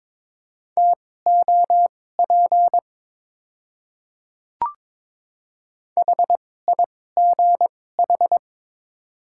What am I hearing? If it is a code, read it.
Morse code, heard as a keyed tone, about 22 wpm, spaced as for 16 wpm, 700 Hz. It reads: TOP HIGH
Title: 2x Character Spacing - Top 500 Words - 22wpm